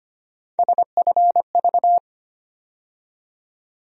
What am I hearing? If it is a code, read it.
Morse code, heard as a keyed tone, about 25 wpm, 700 Hz. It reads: SFV
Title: USA Call Sign Postfixes - 3 Letters 25wpm